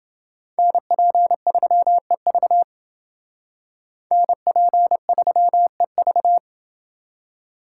Morse code, heard as a keyed tone, about 29 wpm, 700 Hz. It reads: NP3EV NP3EV